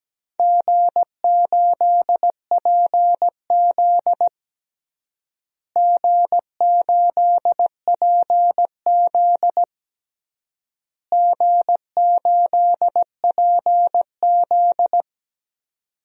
Morse code, heard as a keyed tone, about 17 wpm, 700 Hz. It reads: G8PZ G8PZ G8PZ
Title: DXCC-100 Call Signs - 4 Characters - Repeated 3x - 17wpm